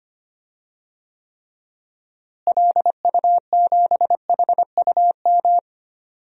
Morse code, heard as a keyed tone, about 25 wpm, 700 Hz. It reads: LU7HUM
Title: DXCC-100 Call Signs - 6 Characters - Repeated 2x - 25wpm